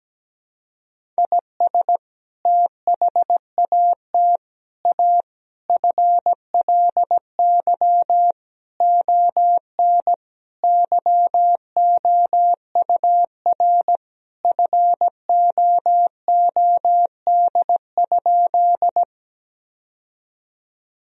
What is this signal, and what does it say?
Morse code, heard as a keyed tone, about 17 wpm, 700 Hz. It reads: IS THAT A FLY ON YOUR FOOD?
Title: Sentences from Top 1000 Words - Review Part 2 - 17wpm